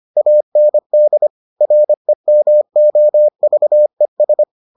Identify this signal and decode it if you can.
Morse code, heard as a keyed tone, about 25 wpm, 600 Hz. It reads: AND REMOVES